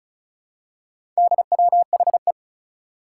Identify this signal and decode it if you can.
Morse code, heard as a keyed tone, about 36 wpm, 700 Hz. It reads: DWHE
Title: Four Challenging Letter-Number Combinations 35wpm